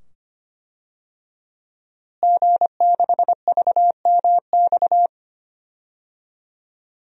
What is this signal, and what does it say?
Morse code, heard as a keyed tone, about 25 wpm, 700 Hz. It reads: G6VMX
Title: DXCC-100 Call Signs - 5 Characters - Repeated 2x - 25wpm